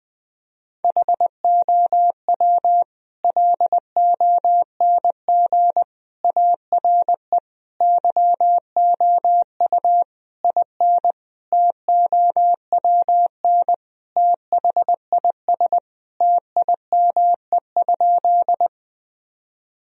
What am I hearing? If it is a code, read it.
Morse code, heard as a keyed tone, about 20 wpm, 700 Hz. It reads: HOW LONG ARE YOU IN TOWN THIS TIME?
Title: Sentences from Top 1000 Words - Review Part 2 - 20wpm